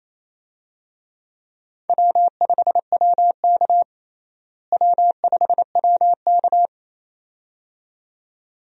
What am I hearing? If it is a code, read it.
Morse code, heard as a keyed tone, about 28 wpm, 700 Hz. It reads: W5WK W5WK
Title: USA Call Signs - 4 Characters - Repeated 2x - 28wpm